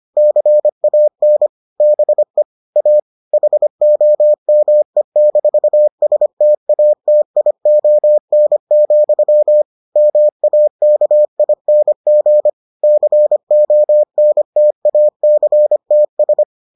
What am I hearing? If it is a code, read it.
Morse code, heard as a keyed tone, about 25 wpm, 600 Hz. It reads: CAN BE A HOME-STATION, MAKING CONTACTS